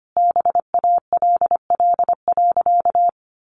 Morse code, heard as a keyed tone, about 25 wpm, 700 Hz. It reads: BALL.
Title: Sentences from Top 900 Words 25wpm